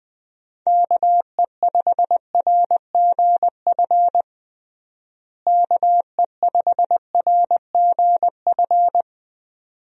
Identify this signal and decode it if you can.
Morse code, heard as a keyed tone, about 20 wpm, 700 Hz. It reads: KE5RGF KE5RGF
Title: USA Call Signs - 6 Characters - Repeated 2x - 20wpm